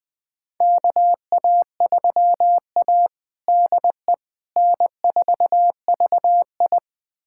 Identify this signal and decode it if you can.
Morse code, heard as a keyed tone, about 20 wpm, 700 Hz. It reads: KA3A DE N4VI
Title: USA Callsign DE Callsign - 4 Characters - 20wpm